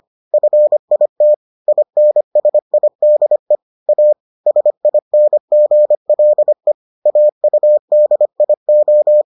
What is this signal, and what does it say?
Morse code, heard as a keyed tone, about 25 wpm, 600 Hz. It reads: FIT INSIDE A SINGLE AUDIO